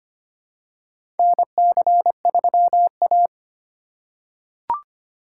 Morse code, heard as a keyed tone, about 25 wpm, 700 Hz. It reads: NC3A
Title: USA Call Signs - 2x Character Spacing - 25wpm